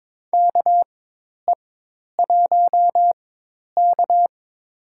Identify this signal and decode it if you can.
Morse code, heard as a keyed tone, about 21 wpm, spaced as for 10 wpm, 700 Hz. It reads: KE1K
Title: USA Call Signs - 4 Characters - 4x Character Spacing - 22wpm